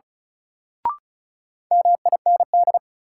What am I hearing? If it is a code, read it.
Morse code, heard as a keyed tone, about 36 wpm, 700 Hz. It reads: MIND
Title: Sentences from Top 500 Words - New Word Drill - 35wpm